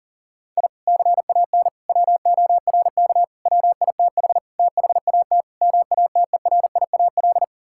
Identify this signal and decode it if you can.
Morse code, heard as a keyed tone, about 40 wpm, 700 Hz. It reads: I CAN WORK WITH THAT MATERIAL
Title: Sentences from Top 1000 Words - Review Part 5 - 40wpm